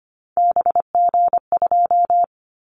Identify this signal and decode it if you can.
Morse code, heard as a keyed tone, about 25 wpm, 700 Hz. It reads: BG2
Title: Callsign Prefixes for Top 100 DXCC - 25wpm